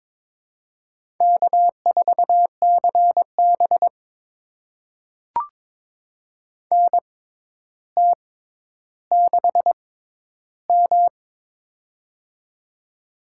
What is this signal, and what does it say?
Morse code, heard as a keyed tone, about 22 wpm, spaced as for 11 wpm, 700 Hz. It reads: K4CB NT6M
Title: USA Call Signs - 4 Characters - 6x Character Spacing - 22wpm